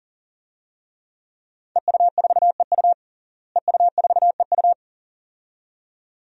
Morse code, heard as a keyed tone, about 39 wpm, 700 Hz. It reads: EU4EU EU4EU
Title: DXCC-100 Call Signs - 5 Characters - Repeated 2x - 40wpm